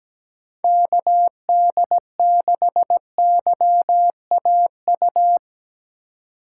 Morse code, heard as a keyed tone, about 17 wpm, 700 Hz. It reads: KD6YAU
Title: USA Call Signs - 6 Characters - 2x Character Spacing - 17wpm